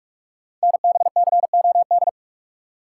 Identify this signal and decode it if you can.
Morse code, heard as a keyed tone, about 44 wpm, 690 Hz. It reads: NBCOD
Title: Code Groups - 5 Letters - 45wpm